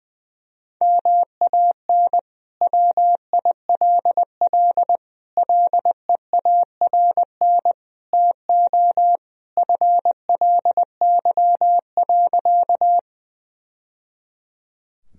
Morse code, heard as a keyed tone, about 20 wpm, 700 Hz. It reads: MAN WILL LEARN TO FLY.
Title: Sentences from Top 1000 Words - Review Part 2 - 20wpm